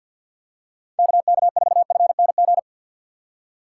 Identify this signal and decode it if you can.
Morse code, heard as a keyed tone, about 50 wpm, 700 Hz. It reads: KK4FNG
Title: USA Call Signs - 6 Characters - 6x Character Spacing - 50wpm